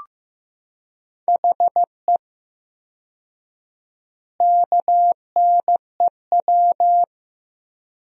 Morse code, heard as a keyed tone, about 15 wpm, 700 Hz. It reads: HE KNEW
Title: 4x Word-Spacing - Sentences from Top 500 Words - 15wpm